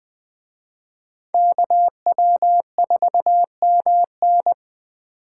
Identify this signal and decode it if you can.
Morse code, heard as a keyed tone, about 24 wpm, 700 Hz. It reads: KW4MN
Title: Speed Racing - USA 2x2 Call Signs 20wpm